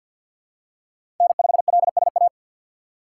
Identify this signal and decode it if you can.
Morse code, heard as a keyed tone, about 51 wpm, 700 Hz. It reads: N5LSA
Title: USA Call Signs - 5 Characters - 8x Character Spacing - 50wpm